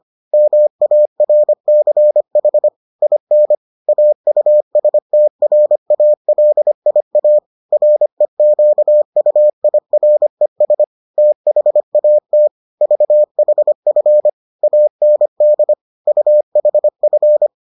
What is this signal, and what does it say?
Morse code, heard as a keyed tone, about 25 wpm, 600 Hz. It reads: MARCH IN AUSTRALIA REQUIRES THAT VHF AND UHF